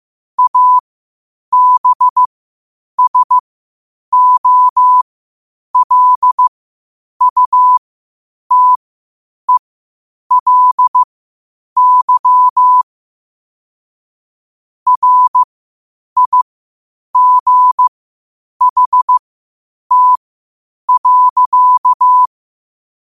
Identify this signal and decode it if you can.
Morse code, heard as a keyed tone, about 15 wpm, spaced as for 8 wpm, 1000 Hz. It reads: ABSOLUTELY RIGHT.
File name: qotdfiles / 8wpm